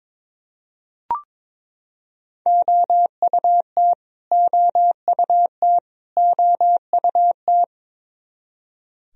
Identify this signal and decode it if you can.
Morse code, heard as a keyed tone, about 22 wpm, 700 Hz. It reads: OUT OUT OUT
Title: Top 100 Words Repeated 22wpm